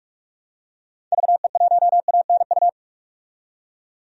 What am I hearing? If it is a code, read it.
Morse code, heard as a keyed tone, about 46 wpm, 700 Hz. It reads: VE1ANU
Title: DXCC-100 Call Signs - 6 Characters - 45wpm